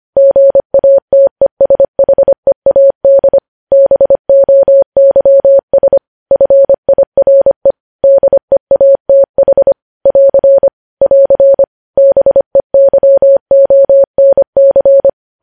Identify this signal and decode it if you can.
Morse code, heard as a keyed tone, about 25 wpm, 550 Hz. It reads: GATESHEAD BOYS FIRE DEATH <AR> <AR> BEYONC
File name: podcast / News headlines-1x-25x25